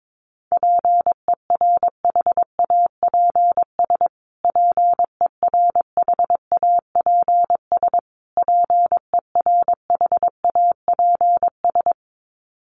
Morse code, heard as a keyed tone, about 22 wpm, 700 Hz. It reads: PERHAPS PERHAPS PERHAPS
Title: Top 500 Words Repeated 22wpm